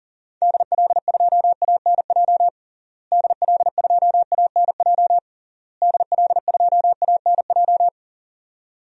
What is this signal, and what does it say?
Morse code, heard as a keyed tone, about 40 wpm, 700 Hz. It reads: DL2ANJ DL2ANJ DL2ANJ
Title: DXCC-100 Call Signs - 6 Characters - Repeated 3x - 40wpm